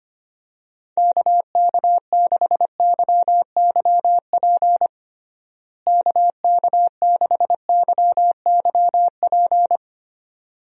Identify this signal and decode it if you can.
Morse code, heard as a keyed tone, about 25 wpm, 700 Hz. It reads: KK6YYP KK6YYP
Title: USA Call Signs - 6 Characters - Repeated 2x - 25wpm